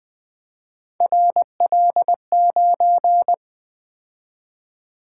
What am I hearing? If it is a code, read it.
Morse code, heard as a keyed tone, about 25 wpm, 700 Hz. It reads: RL9